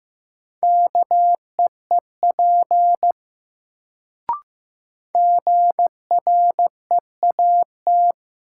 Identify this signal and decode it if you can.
Morse code, heard as a keyed tone, about 15 wpm, 700 Hz. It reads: KEEP GREAT
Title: Sets of 2 Words - New Word Drill - 15wpm